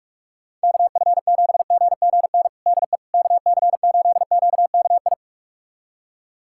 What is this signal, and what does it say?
Morse code, heard as a keyed tone, about 46 wpm, 700 Hz. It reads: KF7GGN DE KC8QKI
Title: USA Callsign DE Callsign 45wpm